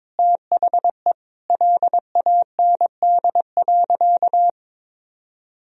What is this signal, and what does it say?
Morse code, heard as a keyed tone, about 22 wpm, 700 Hz. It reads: THE LAND.
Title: Sentences from Top 1000 Words - Review Part 1 - 22wpm